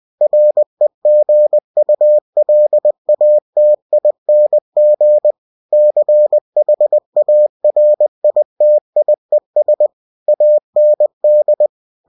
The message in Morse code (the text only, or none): REGULATING CHARITIES AND